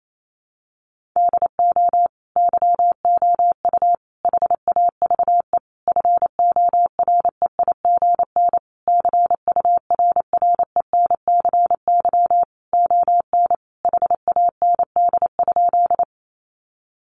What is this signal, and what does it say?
Morse code, heard as a keyed tone, about 28 wpm, 700 Hz. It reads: DO YOU HAVE FOREIGN CURRENCY ON HAND?
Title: Sentences from Top 1500 Words 28wpm